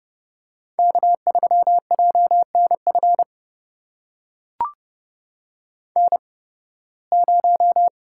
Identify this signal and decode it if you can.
Morse code, heard as a keyed tone, about 31 wpm, spaced as for 20 wpm, 700 Hz. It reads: K3JNF N0
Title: USA Call Signs - 5 Characters - 8x Character Spacing - 30wpm